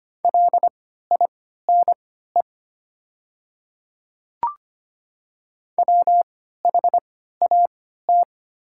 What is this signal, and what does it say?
Morse code, heard as a keyed tone, about 25 wpm, spaced as for 14 wpm, 690 Hz. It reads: LINE WHAT
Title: 3x Character Spacing - Top 200 Words - 25wpm